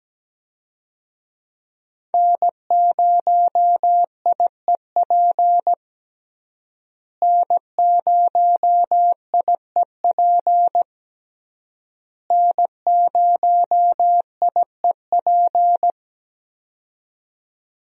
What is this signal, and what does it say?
Morse code, heard as a keyed tone, about 17 wpm, 700 Hz. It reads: N0IEP N0IEP N0IEP
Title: USA Call Signs - 5 Characters - Repeated 3x - 17wpm